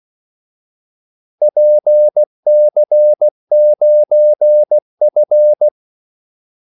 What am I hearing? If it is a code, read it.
Morse code, heard as a keyed tone, about 16 wpm, 600 Hz. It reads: PC9F
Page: Kaldesignaler 16 wpm